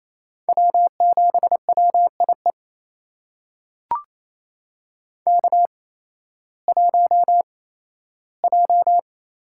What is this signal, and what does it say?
Morse code, heard as a keyed tone, about 28 wpm, spaced as for 15 wpm, 700 Hz. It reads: W7WIE K1J
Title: USA Call Signs - 5 Characters - 8x Character Spacing - 28wpm